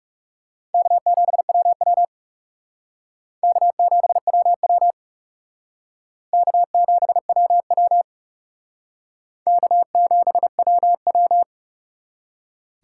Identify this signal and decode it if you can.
Morse code, heard as a keyed tone, about 37 wpm, 700 Hz. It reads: K7WW K7WW K7WW K7WW
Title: Speed Racing - USA 1x2 Call Signs 30wpm